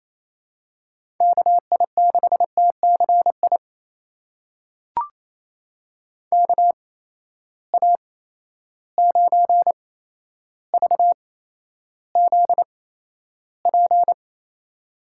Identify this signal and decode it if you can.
Morse code, heard as a keyed tone, about 28 wpm, spaced as for 12 wpm, 700 Hz. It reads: KI6TCI KA9VZP
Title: USA Call Signs - 6 Characters - 8x Character Spacing - 28wpm